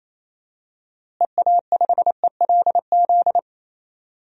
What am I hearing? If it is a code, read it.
Morse code, heard as a keyed tone, about 28 wpm, 700 Hz. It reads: EA5ELZ
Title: DXCC-100 Call Signs - 6 Characters - Repeated 3x - 28wpm